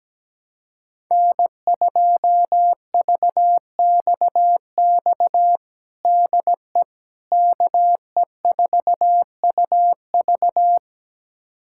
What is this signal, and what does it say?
Morse code, heard as a keyed tone, about 17 wpm, 700 Hz. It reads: N2VXX DE KE4UV
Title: USA Callsign DE Callsign - 5 Characters - 17wpm